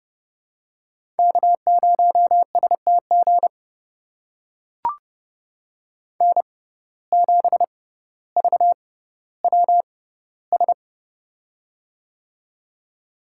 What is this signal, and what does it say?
Morse code, heard as a keyed tone, about 29 wpm, spaced as for 15 wpm, 700 Hz. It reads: K0STG N7VWS
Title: USA Call Signs - 5 Characters - 6x Character Spacing - 30wpm